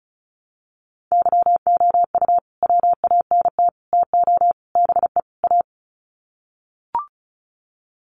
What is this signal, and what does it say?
Morse code, heard as a keyed tone, about 34 wpm, 700 Hz. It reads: YOU WANT TO BE A
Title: Sets of 5 Words Encore2 35wpm